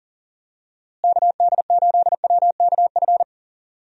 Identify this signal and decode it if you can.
Morse code, heard as a keyed tone, about 40 wpm, 700 Hz. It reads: KD8WKF
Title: USA Call Signs - 6 Characters - Repeated 3x - 40wpm